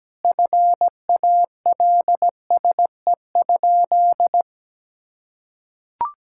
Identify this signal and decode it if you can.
Morse code, heard as a keyed tone, about 17 wpm, 700 Hz. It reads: FALSE?
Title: Sentences from Top 1600 Words 17wpm